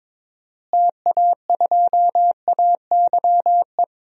Morse code, heard as a keyed tone, about 22 wpm, 700 Hz. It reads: TA2AYE
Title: DXCC-100 Call Signs - 6 Characters - Repeated 2x - 22wpm